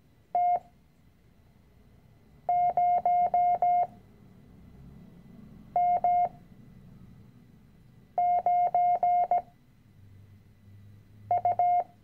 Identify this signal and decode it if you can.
Morse code, heard as a keyed tone, about 17 wpm, spaced as for 4 wpm, 710 Hz. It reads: T0M9U
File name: CW-Online-Kurs / Anfänger / Lektion 010